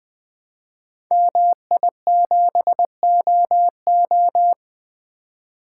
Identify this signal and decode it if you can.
Morse code, heard as a keyed tone, about 20 wpm, 700 Hz. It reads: MI7OO
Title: Code Groups - 5 Letter:Numbers - 20wpm